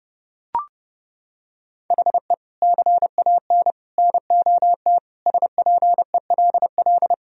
Morse code, heard as a keyed tone, about 29 wpm, 700 Hz. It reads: HE CAN NOT SPELL
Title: Sentences from Top 1000 Words - Review Part 1 - 30wpm